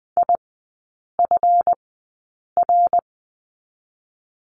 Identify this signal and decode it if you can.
Morse code, heard as a keyed tone, about 20 wpm, 690 Hz. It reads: I F R
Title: 145 Lesson 24e - Letter y - Three Characters 20wpm